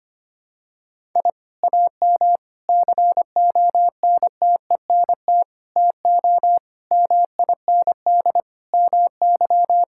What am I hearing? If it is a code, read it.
Morse code, heard as a keyed tone, about 25 wpm, 700 Hz. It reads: I AM CONTENT TO MIND MY
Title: Sentences from Top 1000 Words - Review Part 6 - 25wpm